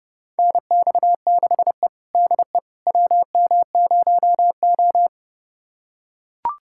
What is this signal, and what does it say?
Morse code, heard as a keyed tone, about 32 wpm, 700 Hz. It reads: NX6E DE WM0O
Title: USA Callsign DE Callsign - 4 Characters - 30wpm